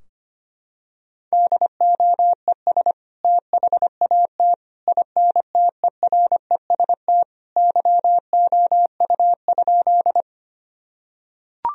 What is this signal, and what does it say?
Morse code, heard as a keyed tone, about 25 wpm, 700 Hz. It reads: DOES THAT INTEREST YOU?
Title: Sentences from Top 1000 Words - Review Part 2 - 25wpm